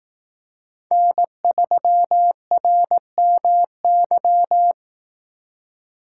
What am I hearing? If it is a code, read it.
Morse code, heard as a keyed tone, about 18 wpm, 700 Hz. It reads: N3RMY
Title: Speed Racing - USA 1x3 Call Signs 15wpm